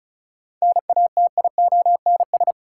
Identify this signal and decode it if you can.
Morse code, heard as a keyed tone, about 36 wpm, 700 Hz. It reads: NATIONS
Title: Sets of 3 Words - New Word Drill - 35wpm